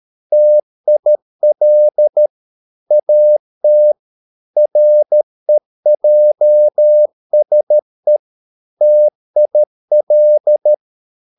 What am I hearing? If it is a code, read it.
Morse code, heard as a keyed tone, about 13 wpm, 600 Hz. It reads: TIL AT REJSE TIL
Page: Ferie 13 wpm